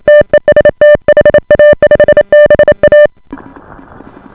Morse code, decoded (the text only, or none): TEST HA5BA